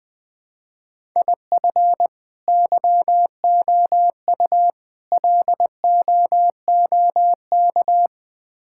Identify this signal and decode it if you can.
Morse code, heard as a keyed tone, about 20 wpm, 700 Hz. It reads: IF YOU LOOK
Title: Build A Phrase - 5-Word Phrases - 20wpm